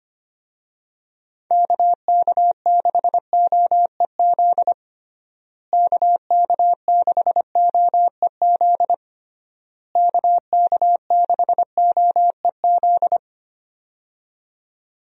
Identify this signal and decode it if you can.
Morse code, heard as a keyed tone, about 25 wpm, 700 Hz. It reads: KK6OEZ KK6OEZ KK6OEZ
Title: USA Call Signs - 6 Characters - Repeated 3x - 25wpm